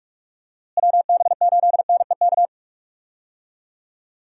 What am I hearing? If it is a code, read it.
Morse code, heard as a keyed tone, about 46 wpm, 700 Hz. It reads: WB8NEK
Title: USA Call Signs - 6 Characters - Repeated 2x - 45wpm